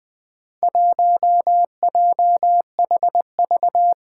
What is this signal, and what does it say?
Morse code, heard as a keyed tone, about 20 wpm, 700 Hz. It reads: 1JHV
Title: Four Challenging Letter-Number Combinations 20wpm